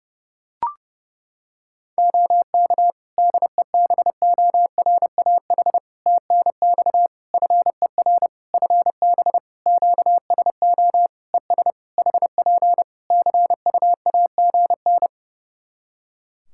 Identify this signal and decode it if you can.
Morse code, heard as a keyed tone, about 32 wpm, 700 Hz. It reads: OK DEBORAH TNX FER FB QSO ES HP CUAGN
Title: Practice QSO - Exchange 4 - Ending - 30wpm